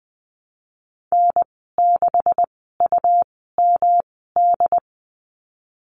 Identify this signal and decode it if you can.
Morse code, heard as a keyed tone, about 20 wpm, spaced as for 14 wpm, 700 Hz. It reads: N6UMD
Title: 167 Lesson 27g - Number 7 - Call Signs 20wpm